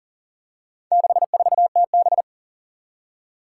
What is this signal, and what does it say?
Morse code, heard as a keyed tone, about 40 wpm, 700 Hz. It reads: 64TB
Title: Four Challenging Letter-Number Combinations 40wpm